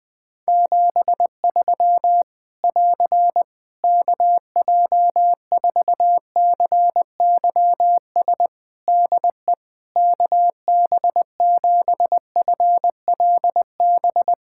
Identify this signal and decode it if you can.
Morse code, heard as a keyed tone, about 20 wpm, 700 Hz. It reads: 73 <AR> KJ4CYS DE KB7FLB
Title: Practice QSO - Exchange 4 - Ending - 20wpm